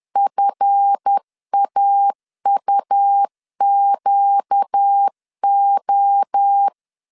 Morse code, heard as a keyed tone, about 10 wpm, 800 Hz. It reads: FAUQO
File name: CW / DET / 1 9 1500C S B 10wpm 4439